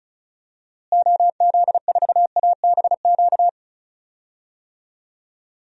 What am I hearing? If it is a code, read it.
Morse code, heard as a keyed tone, about 36 wpm, 700 Hz. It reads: OZ4ABQ
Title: DXCC-100 Call Signs - 6 Characters - Repeated 2x - 35wpm